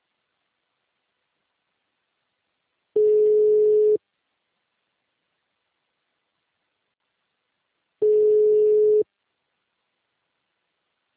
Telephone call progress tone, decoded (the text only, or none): ringback tone